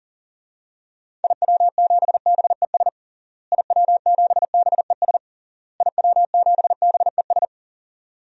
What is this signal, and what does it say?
Morse code, heard as a keyed tone, about 40 wpm, 700 Hz. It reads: IW7BES IW7BES IW7BES
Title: DXCC-100 Call Signs - 6 Characters - Repeated 3x - 40wpm